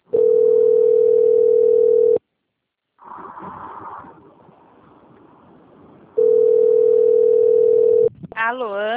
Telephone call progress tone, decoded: ringback tone

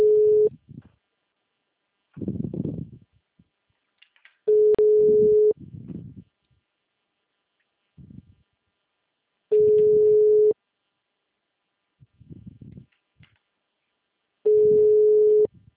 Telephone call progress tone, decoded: ringback tone

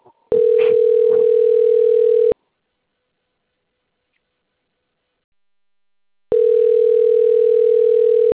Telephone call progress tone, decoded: ringback tone

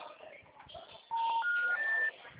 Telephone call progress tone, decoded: special information tone

